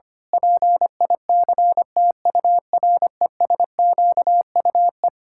Morse code, heard as a keyed tone, about 25 wpm, 700 Hz. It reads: PICTURESQUE
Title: Pangram Challenge v2 - 25wpm